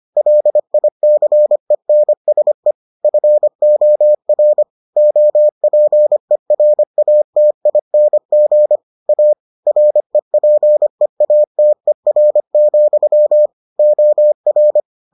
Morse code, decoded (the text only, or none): LICENSE FOR OPERATING A REPEATER, OR